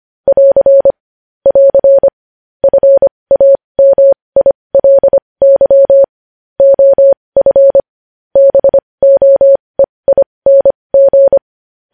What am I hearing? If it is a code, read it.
Morse code, heard as a keyed tone, about 25 wpm, spaced as for 20 wpm, 550 Hz. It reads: <AR> <AR> FAMILY OF BOEING
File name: podcast / News headlines-1x-25x20